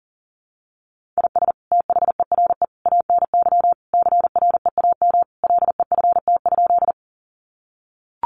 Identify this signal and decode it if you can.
Morse code, heard as a keyed tone, about 40 wpm, 700 Hz. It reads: IS THERE ANY CREAM LEFT?
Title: Sentences from Top 1500 Words 40wpm